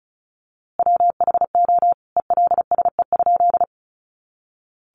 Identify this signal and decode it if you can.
Morse code, heard as a keyed tone, about 34 wpm, 700 Hz. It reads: WHO ELSE?
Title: A Sound Of Thunder 35wpm